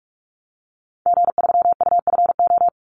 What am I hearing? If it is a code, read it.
Morse code, heard as a keyed tone, about 44 wpm, 700 Hz. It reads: G3UFO
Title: Call Signs Repeated 3x - 45wpm